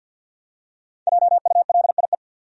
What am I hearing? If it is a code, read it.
Morse code, heard as a keyed tone, about 46 wpm, 700 Hz. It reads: JULIE
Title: USA Common Names - Max 5 Letters - 50wpm